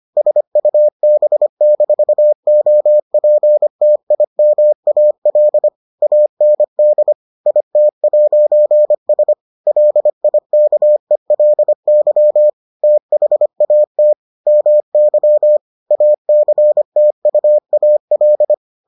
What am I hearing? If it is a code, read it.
Morse code, heard as a keyed tone, about 25 wpm, 600 Hz. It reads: SUB-OPTIMAL AND IT'S LIKELY THAT MY ACTUAL